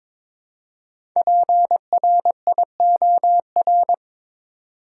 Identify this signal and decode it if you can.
Morse code, heard as a keyed tone, about 22 wpm, 700 Hz. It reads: PRIOR
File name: 5 Letter Words - First 3 Letters Given - 22wpm